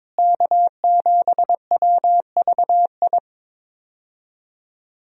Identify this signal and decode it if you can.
Morse code, heard as a keyed tone, about 22 wpm, 700 Hz. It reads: K7WVI
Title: USA Call Signs - 5 Characters - Repeated 2x - 22wpm